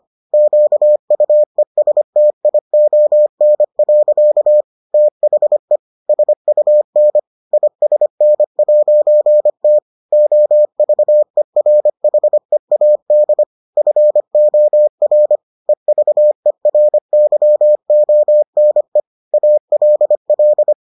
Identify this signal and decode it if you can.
Morse code, heard as a keyed tone, about 25 wpm, 600 Hz. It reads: QUESTION. THE SUN ISN'T OVERHEAD FOR EVERYONE ALL